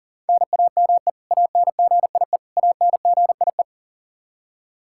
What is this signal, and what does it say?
Morse code, heard as a keyed tone, about 40 wpm, 700 Hz. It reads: NAME ANGIE ANGIE
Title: Practice QSO - Exchange 1 - Beginning - 40wpm